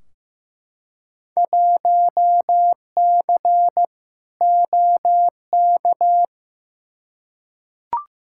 Morse code, heard as a keyed tone, about 15 wpm, 700 Hz. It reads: 1C OK
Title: Field Day Exchanges Encore - 15wpm